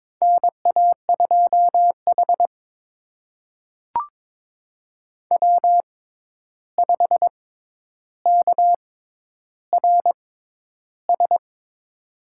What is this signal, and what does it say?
Morse code, heard as a keyed tone, about 22 wpm, spaced as for 10 wpm, 700 Hz. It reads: NA2H W5KRS
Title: USA Call Signs - 6x Character Spacing - 22wpm